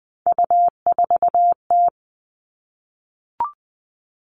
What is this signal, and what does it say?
Morse code, heard as a keyed tone, about 20 wpm, 700 Hz. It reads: U4T
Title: All Cut Numbers - 3-Digit 20wpm